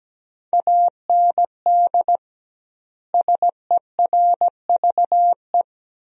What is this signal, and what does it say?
Morse code, heard as a keyed tone, about 17 wpm, 700 Hz. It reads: AND SERVE
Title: 2x Word-Spacing - Sentences from Top 400 Words - 17wpm